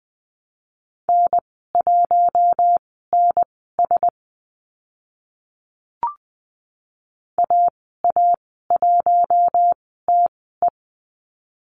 Morse code, heard as a keyed tone, about 20 wpm, spaced as for 14 wpm, 700 Hz. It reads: N1NS AA1TE
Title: 042 Lesson 8g - Number 1 - Call Signs 20wpm